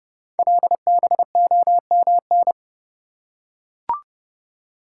Morse code, heard as a keyed tone, about 30 wpm, 700 Hz. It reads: LBOMN
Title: Code Groups - 5 Letters - 30wpm